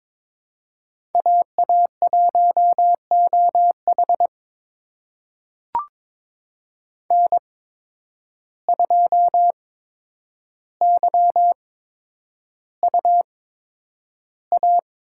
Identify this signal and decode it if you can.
Morse code, heard as a keyed tone, about 22 wpm, spaced as for 9 wpm, 700 Hz. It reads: AA1OH N2YUA
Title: USA Call Signs - 5 Characters - 8x Character Spacing - 22wpm